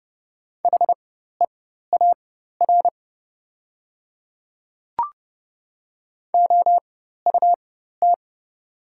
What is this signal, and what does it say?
Morse code, heard as a keyed tone, about 30 wpm, spaced as for 14 wpm, 700 Hz. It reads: HEAR OUT
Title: 4x Character Spacing - Top 300 Words - 30wpm